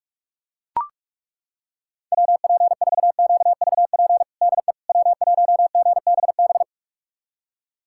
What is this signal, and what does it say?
Morse code, heard as a keyed tone, about 42 wpm, 700 Hz. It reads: WP4QVP DE W1GBB
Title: USA Callsign DE Callsign 45wpm